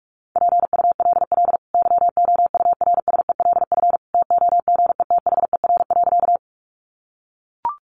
Morse code, heard as a keyed tone, about 44 wpm, 700 Hz. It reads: PULL YOURSELF TOGETHER.
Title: Sentences from Top 600 Words 45wpm